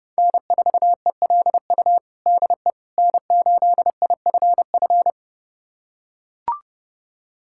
Morse code, heard as a keyed tone, about 30 wpm, 700 Hz. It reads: N4ELU DE N8IFF
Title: USA Callsign DE Callsign - 5 Characters - 30wpm